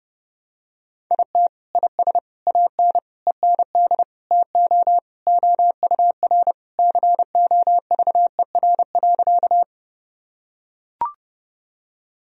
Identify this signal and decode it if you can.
Morse code, heard as a keyed tone, about 30 wpm, 700 Hz. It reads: IT IS AN END TO OUR COVER.
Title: Sentences from Top 1000 Words - Review Part 1 - 30wpm